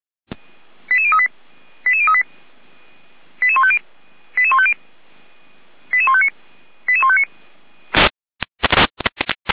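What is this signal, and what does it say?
Five-tone selective call (ZVEI), heard as quick runs of five tones, tones of about 70 ms. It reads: 88038 88038 88150 88150 88159 88159
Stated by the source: ZVEI five-tone selective call received on SDR, 455 channel path.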